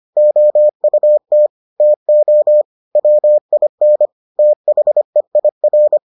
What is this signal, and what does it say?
Morse code, heard as a keyed tone, about 25 wpm, 600 Hz. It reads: OUT TO WIN THEIR